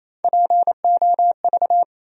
Morse code, heard as a keyed tone, about 28 wpm, 700 Hz. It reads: POV
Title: USA Call Sign Postfixes - 3 Letters 28wpm